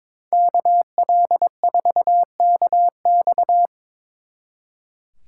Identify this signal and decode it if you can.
Morse code, heard as a keyed tone, about 24 wpm, 700 Hz. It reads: KL4KX
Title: Speed Racing - USA Call Signs 22wpm